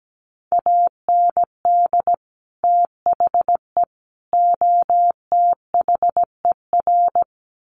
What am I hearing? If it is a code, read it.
Morse code, heard as a keyed tone, about 17 wpm, 700 Hz. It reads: AND THE OTHER
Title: Sets of 3 Words 17wpm